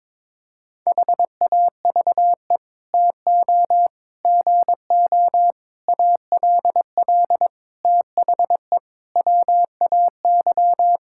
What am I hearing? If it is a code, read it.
Morse code, heard as a keyed tone, about 22 wpm, 700 Hz. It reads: HAVE TO GO ALL THE WAY
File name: practice / 22wpm / Sets of 8 Words - 22wpm